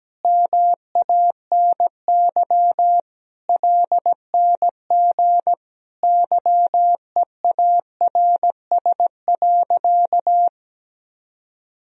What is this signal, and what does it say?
Morse code, heard as a keyed tone, about 17 wpm, 700 Hz. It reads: MANY LNG YEARS.
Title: Sentences from Top 200 Words and Abbreviations 17wpm